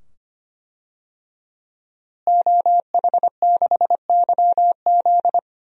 Morse code, heard as a keyed tone, about 25 wpm, 700 Hz. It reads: OH6YZ
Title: DXCC-100 Call Signs - 5 Characters - Repeated 2x - 25wpm